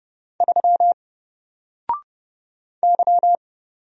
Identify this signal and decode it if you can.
Morse code, heard as a keyed tone, about 31 wpm, 700 Hz. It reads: 3 Y
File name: Single-Letter-Number-30wpm